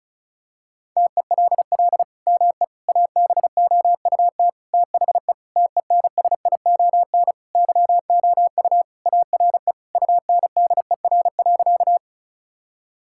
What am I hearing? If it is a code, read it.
Morse code, heard as a keyed tone, about 36 wpm, 700 Hz. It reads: TELL ME ABOUT THE TENSION YOU ARE UNDER.
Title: Sentences from Top 1600 Words 35wpm